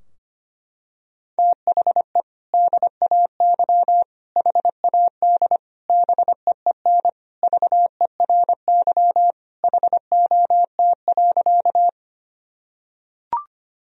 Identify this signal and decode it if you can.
Morse code, heard as a keyed tone, about 25 wpm, 700 Hz. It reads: THE DAY HAD BEEN VERY HOT.
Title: Sentences from Top 1000 Words - Review Part 3 - 25wpm